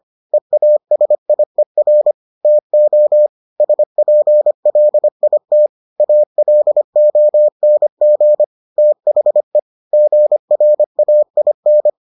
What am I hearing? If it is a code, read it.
Morse code, heard as a keyed tone, about 25 wpm, 600 Hz. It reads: EASIER TO SPLIT ALONG THE GRAIN